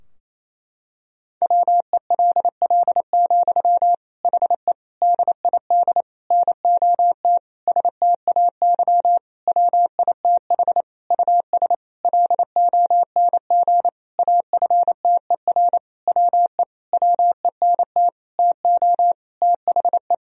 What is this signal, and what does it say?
Morse code, heard as a keyed tone, about 28 wpm, 700 Hz. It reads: WELL, HE DID NOT STAY WITH US LONG AFTER WE WENT TO THE
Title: Sentences from Top 1000 Words - Review Part 2 - 28wpm